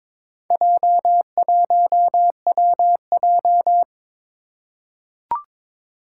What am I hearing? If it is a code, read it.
Morse code, heard as a keyed tone, about 22 wpm, 700 Hz. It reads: J1WJ
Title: Four Challenging Letter-Number Combinations 22wpm